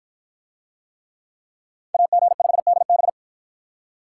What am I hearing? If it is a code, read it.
Morse code, heard as a keyed tone, about 53 wpm, 700 Hz. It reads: AG5DB